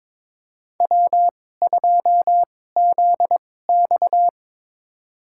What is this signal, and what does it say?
Morse code, heard as a keyed tone, about 22 wpm, spaced as for 16 wpm, 700 Hz. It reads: W2ZX
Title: USA Call Signs - 4 Characters - 2x Character Spacing - 22wpm